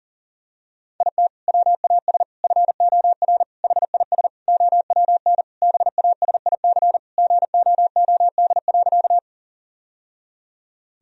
Morse code, heard as a keyed tone, about 40 wpm, 700 Hz. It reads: IT WAS FOR HIS OWN BASIC GOOD.
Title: Sentences from Top 1000 Words - Review Part 3 - 40wpm